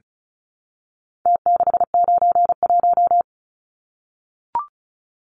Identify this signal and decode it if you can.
Morse code, heard as a keyed tone, about 34 wpm, 700 Hz. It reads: T691
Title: Numbers - Contest Style - 4-Digit 35wpm